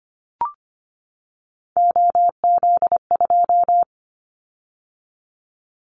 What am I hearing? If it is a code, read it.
Morse code, heard as a keyed tone, about 25 wpm, 700 Hz. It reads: OZ2
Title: Callsign Prefixes for Top 100 DXCC - 25wpm